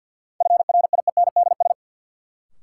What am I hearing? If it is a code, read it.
Morse code, heard as a keyed tone, about 52 wpm, 700 Hz. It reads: FRIENDS
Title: Sentences from Top 1600 Words - New Word Drill Alt - 50wpm